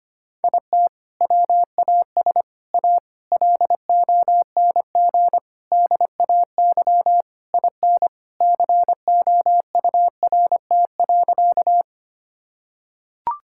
Morse code, heard as a keyed tone, about 25 wpm, 700 Hz. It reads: IT WAS A LONG DAY IN COURT.
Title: Sentences from Top 1600 Words 25wpm